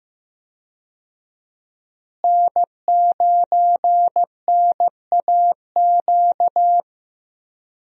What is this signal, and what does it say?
Morse code, heard as a keyed tone, about 15 wpm, 700 Hz. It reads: N9NAQ
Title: USA Call Signs - 5 Characters - Repeated 3x - 15wpm